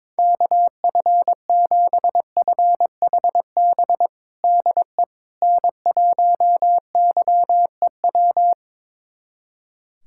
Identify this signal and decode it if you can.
Morse code, heard as a keyed tone, about 22 wpm, 700 Hz. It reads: KF7FHB DE N1YEW
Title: USA Callsign DE Callsign 22wpm